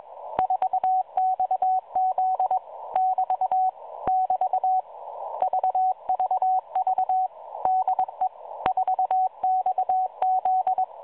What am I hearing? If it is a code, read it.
Morse code, heard as a keyed tone, about 21 wpm, 750 Hz. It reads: 4XZ <BT> <BT> VVV DE 4XZ
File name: 4XZ Shazepe